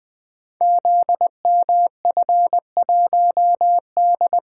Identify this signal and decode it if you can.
Morse code, heard as a keyed tone, about 21 wpm, 700 Hz. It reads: ZMF1D
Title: Code Groups - 5 Letter:Numbers - 20wpm